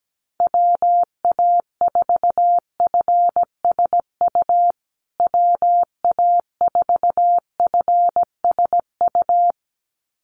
Morse code, heard as a keyed tone, about 17 wpm, 700 Hz. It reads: WA4FSU WA4FSU
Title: USA Call Signs Repeated 2x 17wpm